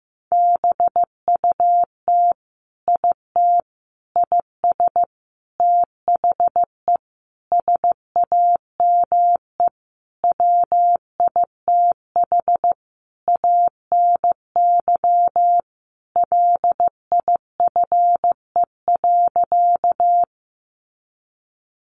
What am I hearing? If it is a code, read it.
Morse code, heard as a keyed tone, about 15 wpm, 700 Hz. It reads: BUT IT IS THE SAME WITH ANY LIFE.
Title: Sentences from Top 300 Words 15wpm